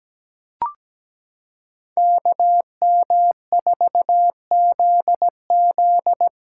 Morse code, heard as a keyed tone, about 17 wpm, 700 Hz. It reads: KM4ZZ
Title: K1USN SST QSO Practice - 17wpm